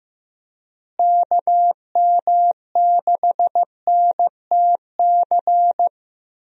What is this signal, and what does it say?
Morse code, heard as a keyed tone, about 15 wpm, 700 Hz. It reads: KM6NTC
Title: USA Call Signs - 6 Characters - Repeated 3x - 15wpm